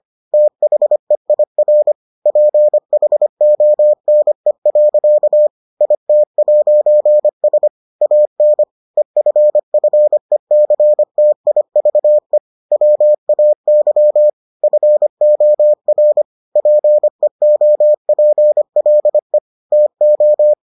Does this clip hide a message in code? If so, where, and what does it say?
Yes: Morse code from 0.3 to 20.5 s, heard as a keyed tone, about 25 wpm, 600 Hz. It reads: THEIR PHONE. IT'S AN EFFECTIVE WAY FOR PEOPLE TO